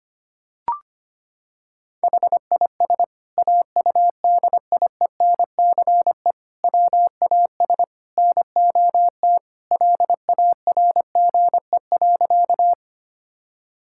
Morse code, heard as a keyed tone, about 25 wpm, 700 Hz. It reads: HIS AUDIENCE WAS NOT LARGE.
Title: Sentences from Top 1000 Words - Review Part 5 - 25wpm